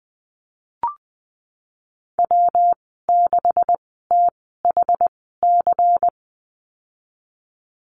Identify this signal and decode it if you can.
Morse code, heard as a keyed tone, about 20 wpm, spaced as for 15 wpm, 700 Hz. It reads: W6THC
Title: 128 Lesson 21g - Number 6 - Call Signs 20wpm